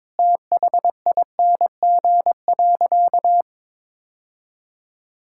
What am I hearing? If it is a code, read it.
Morse code, heard as a keyed tone, about 22 wpm, 700 Hz. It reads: THING.
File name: practice / 22wpm / Sentences with 3-Word Phrases - 22wpm